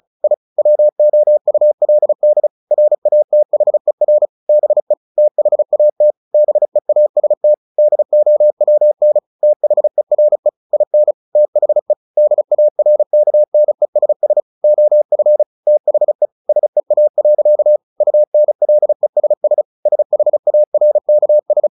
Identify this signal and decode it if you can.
Morse code, heard as a keyed tone, about 34 wpm, 600 Hz. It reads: I WOULD RATHER BE THAT BEAST DOWN THERE IN THE DARKNESS OF THE SEA. UNLESS SHARKS
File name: the old man and the sea / the old man and the sea 1 35wpm